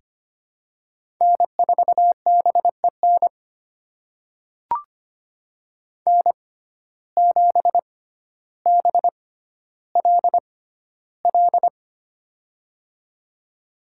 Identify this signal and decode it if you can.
Morse code, heard as a keyed tone, about 25 wpm, spaced as for 13 wpm, 700 Hz. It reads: N4BEN N7BLL
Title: USA Call Signs - 5 Characters - 6x Character Spacing - 25wpm